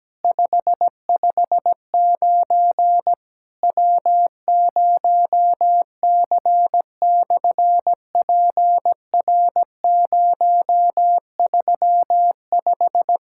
Morse code, heard as a keyed tone, about 17 wpm, 700 Hz. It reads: 559 W0C/PR035
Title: SOTA QSO Practice 17wpm